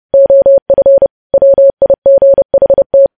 Morse code, heard as a keyed tone, about 30 wpm, 550 Hz. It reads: OF WIGHT
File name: podcast / News headlines-1x-30x30